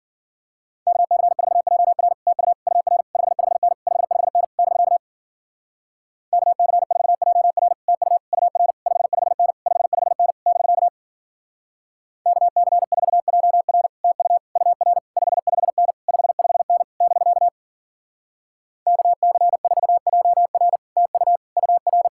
Morse code, heard as a keyed tone, about 53 wpm, 700 Hz. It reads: KC4JR TU UR 55N 55N <BK> KC4JR TU UR 55N 55N <BK> KC4JR TU UR 55N 55N <BK> KC4JR TU UR